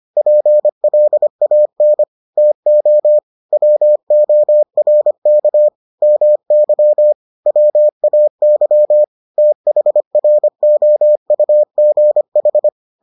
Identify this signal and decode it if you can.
Morse code, heard as a keyed tone, about 25 wpm, 600 Hz. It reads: PLAN TO WORK MY WAY THROUGH